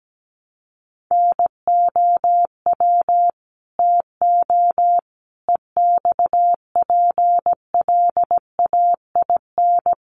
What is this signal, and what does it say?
Morse code, heard as a keyed tone, about 17 wpm, 700 Hz. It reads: NOW TO EXPLAIN
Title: Sentences from Top 1300 Words 17wpm